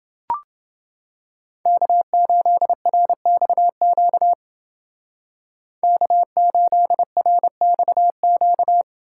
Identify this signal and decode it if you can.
Morse code, heard as a keyed tone, about 28 wpm, 700 Hz. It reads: K8RXQ K8RXQ